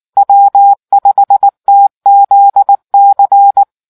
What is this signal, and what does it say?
Morse code, heard as a keyed tone, about 19 wpm, 800 Hz. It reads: W5TZC